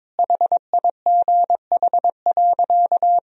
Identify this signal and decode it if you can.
Morse code, heard as a keyed tone, about 22 wpm, 700 Hz. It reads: HIGH.
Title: Sentences from Top 1000 Words - Review Part 2 - 22wpm